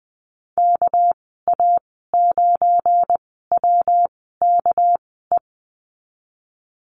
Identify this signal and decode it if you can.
Morse code, heard as a keyed tone, about 20 wpm, spaced as for 15 wpm, 700 Hz. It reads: KA9WKE
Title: 200 Lesson 32g - Letter k - Call Signs 20wpm